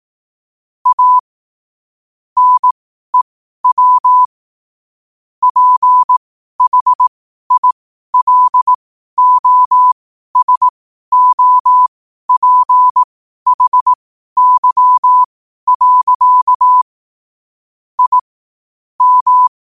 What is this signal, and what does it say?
Morse code, heard as a keyed tone, about 18 wpm, spaced as for 12 wpm, 1000 Hz. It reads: A NEW PHILOSOPHY. I M
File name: qotdfiles / 12wpm